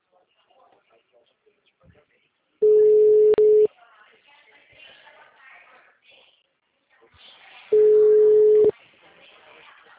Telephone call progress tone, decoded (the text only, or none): ringback tone